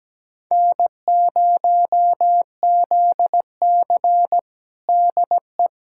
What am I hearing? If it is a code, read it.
Morse code, heard as a keyed tone, about 17 wpm, 700 Hz. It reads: N0ZC DE
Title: USA Callsign DE Callsign - 4 Characters - 17wpm